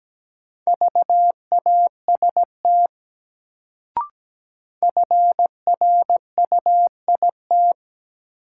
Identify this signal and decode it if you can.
Morse code, heard as a keyed tone, about 17 wpm, 700 Hz. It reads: VAST FRUIT
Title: Sentences from Top 1300 Words - New Word Drill Alt - 17wpm